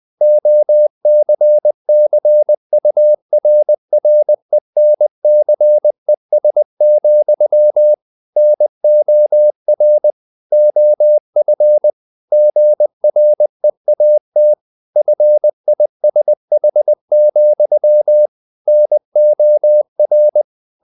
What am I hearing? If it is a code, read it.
Morse code, heard as a keyed tone, about 20 wpm, 600 Hz. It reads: OCCURRENCES, NOR OF GREAT FISH, NOR
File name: the old man and the sea / the old man and the sea part 1 20wpm